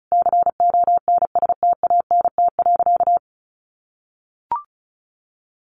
Morse code, heard as a keyed tone, about 34 wpm, 700 Hz. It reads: CONSTANT.
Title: Sentences from Top 1500 Words 35wpm